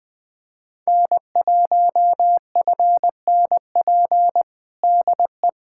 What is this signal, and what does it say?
Morse code, heard as a keyed tone, about 20 wpm, 700 Hz. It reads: N1FNP DE
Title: USA Callsign DE Callsign - 5 Characters - 20wpm